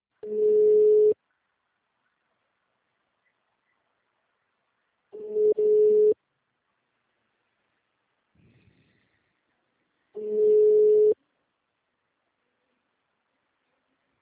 Ringback tone (Europe, 425 Hz).
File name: out- -2002- - - .747